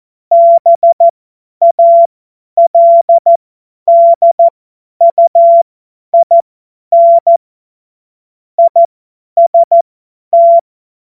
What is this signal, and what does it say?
Morse code, heard as a keyed tone, about 14 wpm, spaced as for 10 wpm, 680 Hz. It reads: BALDUIN IST
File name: Morsekurs / Witze / Witz 25. Januar 2022, 50 F70